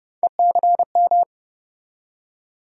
Morse code, heard as a keyed tone, about 30 wpm, 700 Hz. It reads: ECM
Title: USA Call Sign Postfixes - 3 Letters 30wpm